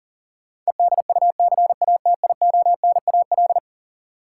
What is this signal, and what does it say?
Morse code, heard as a keyed tone, about 40 wpm, 700 Hz. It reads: EDUCATIONAL
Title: Sentences from Top 1200 Words - New Word Drill Alt - 40wpm